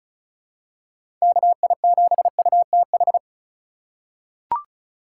Morse code, heard as a keyed tone, about 34 wpm, 700 Hz. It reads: KI7UTH
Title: USA Call Signs - 6 Characters - 4x Character Spacing - 35wpm